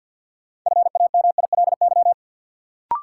Morse code, heard as a keyed tone, about 53 wpm, 700 Hz. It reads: FAMILY
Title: Sentences from Top 400 Words - New Word Drill - 50wpm